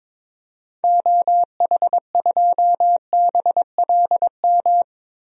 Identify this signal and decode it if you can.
Morse code, heard as a keyed tone, about 22 wpm, 700 Hz. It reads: OH2BLM
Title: DXCC-100 Call Signs - 6 Characters - Repeated 2x - 22wpm